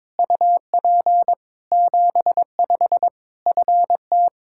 Morse code, heard as a keyed tone, about 21 wpm, 700 Hz. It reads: UP 75 FT
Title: Practice QSO - Exchange 2 - Info - 22wpm